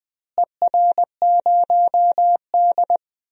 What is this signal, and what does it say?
Morse code, heard as a keyed tone, about 20 wpm, 700 Hz. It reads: ER0D
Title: DXCC-100 Call Signs - 4 Characters - Repeated 3x - 20wpm